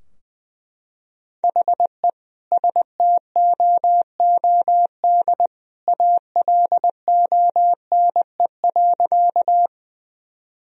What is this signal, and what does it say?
Morse code, heard as a keyed tone, about 20 wpm, 700 Hz. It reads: HE STOOD ALONE.
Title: Sentences from Top 1000 Words - Review Part 2 - 20wpm